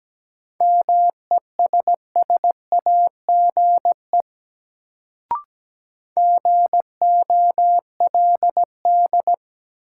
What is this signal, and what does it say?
Morse code, heard as a keyed tone, about 17 wpm, 700 Hz. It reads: MESSAGE GOLD
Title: Sentences from Top 1100 Words - New Word Drill Alt - 17wpm